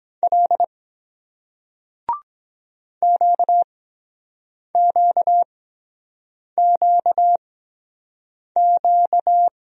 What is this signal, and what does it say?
Morse code, heard as a keyed tone, about 22 wpm, spaced as for 7 wpm, 700 Hz. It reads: L QQQQ